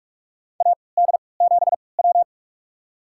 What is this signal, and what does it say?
Morse code, heard as a keyed tone, about 46 wpm, spaced as for 25 wpm, 700 Hz. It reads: AD7W
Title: USA Call Signs - 4 Characters - 3x Character Spacing - 45wpm